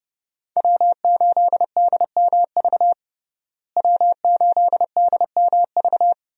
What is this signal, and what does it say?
Morse code, heard as a keyed tone, about 32 wpm, 700 Hz. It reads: W8DMV W8DMV
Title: USA Call Signs - 5 Characters - Repeated 2x - 30wpm